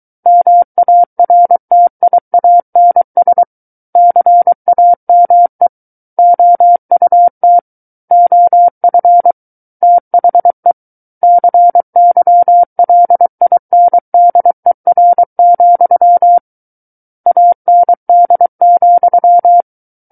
Morse code, heard as a keyed tone, about 23 wpm, 700 Hz. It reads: MARTIANS CAME OUT OF THE CYLINDER, AND,
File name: Book One / 14-Chapter 14 - 23 WPM